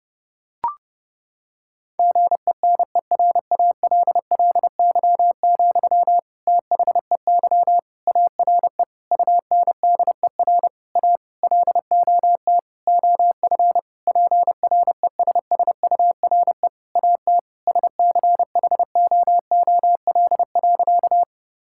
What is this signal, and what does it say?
Morse code, heard as a keyed tone, about 30 wpm, 700 Hz. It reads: GENERALLY, THEY ARE UNDER A LOT OF PRESSURE AT SCHOOL.
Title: Sentences from Top 1000 Words - Review Part 6 - 30wpm